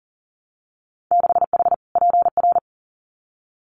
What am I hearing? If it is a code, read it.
Morse code, heard as a keyed tone, about 40 wpm, 700 Hz. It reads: 6H PR
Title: Winter Field Day Exchanges Encore 40wpm